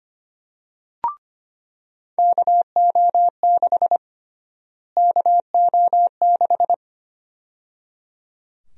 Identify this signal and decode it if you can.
Morse code, heard as a keyed tone, about 25 wpm, 700 Hz. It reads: KO6 KO6
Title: USA Call Sign Prefixes - Repeated 2x - 25wpm